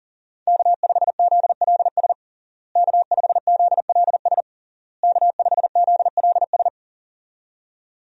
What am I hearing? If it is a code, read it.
Morse code, heard as a keyed tone, about 40 wpm, 700 Hz. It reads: K5ZLS K5ZLS K5ZLS
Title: USA Call Signs - 5 Characters - Repeated 3x - 40wpm